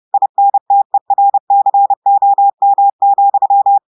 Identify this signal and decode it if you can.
Morse code, heard as a keyed tone, about 30 wpm, 800 Hz. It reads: INTERCOM,
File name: audios / portugais / portugais texte en clair 1 vitesse 30 mots minutes